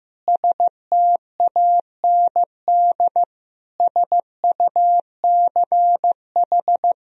Morse code, heard as a keyed tone, about 15 wpm, 700 Hz. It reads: STAND SUCH
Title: Sentences from Top 1000 Words - Review Part 3 - 15wpm